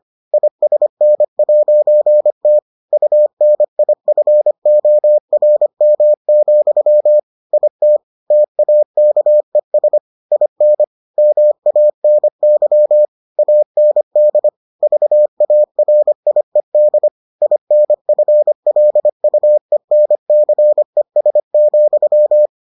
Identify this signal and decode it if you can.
Morse code, heard as a keyed tone, about 25 wpm, 600 Hz. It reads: ISN'T UNIFORM, IT TAKES IN MANY AND VARIED INFLUENCES,